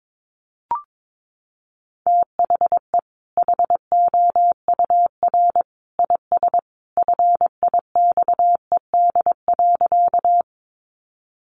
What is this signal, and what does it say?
Morse code, heard as a keyed tone, about 22 wpm, 700 Hz. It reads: THE HOUR IS FIXED.
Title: Sentences from Top 1100 Words 22wpm